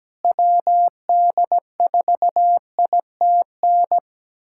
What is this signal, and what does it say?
Morse code, heard as a keyed tone, about 17 wpm, 700 Hz. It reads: WD4ITN
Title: USA Callsign DE Callsign 17wpm